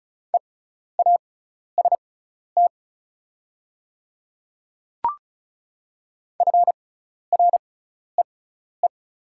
Morse code, heard as a keyed tone, about 34 wpm, spaced as for 12 wpm, 700 Hz. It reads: EAST FREE
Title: 6x Character Spacing - Top 500 Words - 35wpm